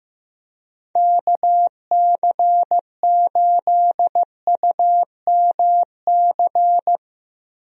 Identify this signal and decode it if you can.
Morse code, heard as a keyed tone, about 15 wpm, 700 Hz. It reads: KC8UMC
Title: USA Call Signs - 6 Characters - 15wpm